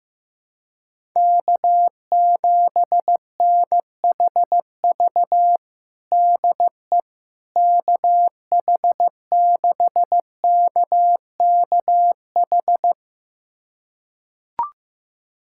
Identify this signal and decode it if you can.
Morse code, heard as a keyed tone, about 15 wpm, 700 Hz. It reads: K7NHV DE KH6KKH
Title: USA Callsign DE Callsign 15wpm